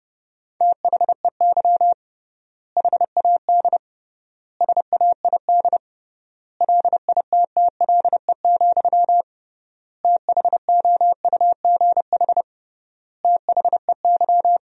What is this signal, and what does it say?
Morse code, heard as a keyed tone, about 30 wpm, 700 Hz. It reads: THEY HAD SAID LITTLE, THOUGH THEY
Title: 3x Word-Spacing - Sentences from Top 400 Words - 30wpm